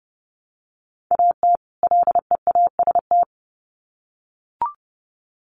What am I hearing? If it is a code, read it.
Morse code, heard as a keyed tone, about 30 wpm, 700 Hz. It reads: AT LEAST
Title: Sets of 2 Words 30wpm